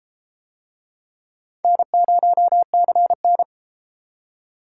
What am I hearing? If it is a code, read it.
Morse code, heard as a keyed tone, about 32 wpm, 700 Hz. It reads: N0CN